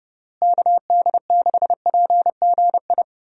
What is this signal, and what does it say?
Morse code, heard as a keyed tone, about 30 wpm, 700 Hz. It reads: KD6PGI
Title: USA Call Signs - 6 Characters - Repeated 3x - 30wpm